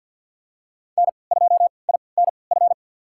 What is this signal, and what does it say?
Morse code, heard as a keyed tone, about 47 wpm, spaced as for 28 wpm, 700 Hz. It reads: N2INF
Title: USA Call Signs - 5 Characters - 3x Character Spacing - 50wpm